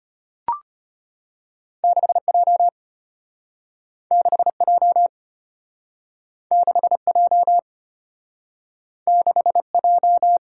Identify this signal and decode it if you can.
Morse code, heard as a keyed tone, about 35 wpm, 700 Hz. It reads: 6J 6J 6J 6J